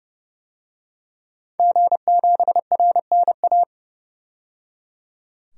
Morse code, heard as a keyed tone, about 30 wpm, 700 Hz. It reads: G7RNA
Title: DXCC-100 Call Signs - 5 Characters - 30wpm